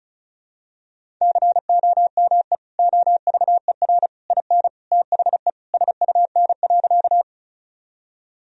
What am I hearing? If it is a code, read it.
Morse code, heard as a keyed tone, about 36 wpm, 700 Hz. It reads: COME OVER IN THE SUN.
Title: Sentences from Top 1000 Words - Review Part 1 - 35wpm